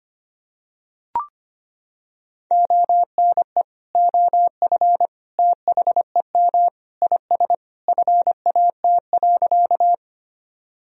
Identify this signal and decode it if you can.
Morse code, heard as a keyed tone, about 25 wpm, 700 Hz. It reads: ONE OF THEM IS FAT.
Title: Sentences from Top 1000 Words - Review Part 4 - 25wpm